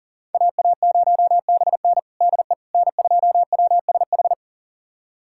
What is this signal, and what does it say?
Morse code, heard as a keyed tone, about 40 wpm, 700 Hz. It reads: AA0BN DE N2WSH
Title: USA Callsign DE Callsign - 5 Characters - 40wpm